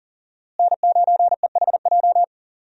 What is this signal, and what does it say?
Morse code, heard as a keyed tone, about 40 wpm, 700 Hz. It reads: N9EHJ
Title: USA Call Signs - 5 Characters - Repeated 2x - 40wpm